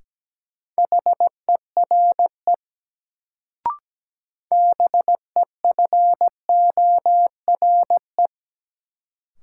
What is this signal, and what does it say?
Morse code, heard as a keyed tone, about 17 wpm, 700 Hz. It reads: HERE BEFORE
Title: Sentences from Top 200 Words - New Word Drill - 17wpm